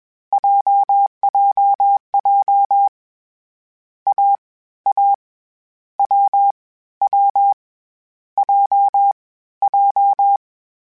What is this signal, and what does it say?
Morse code, heard as a keyed tone, about 21 wpm, 800 Hz. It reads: JJJ A A W W J J
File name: Difficult Combos / Difficult Combos-mp3s / AWJ @20